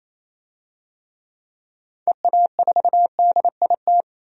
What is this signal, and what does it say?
Morse code, heard as a keyed tone, about 27 wpm, 700 Hz. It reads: EA4DIT
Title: DXCC-100 Call Signs - 6 Characters - 28wpm